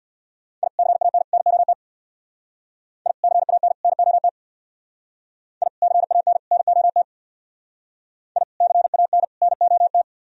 Morse code, heard as a keyed tone, about 64 wpm, 700 Hz. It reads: I CAN NOT I CAN NOT I CAN NOT I CAN NOT